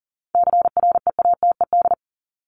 Morse code, heard as a keyed tone, about 40 wpm, 700 Hz. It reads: CREATED
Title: ED Suffix - Top 100 Words with Base Words 40wpm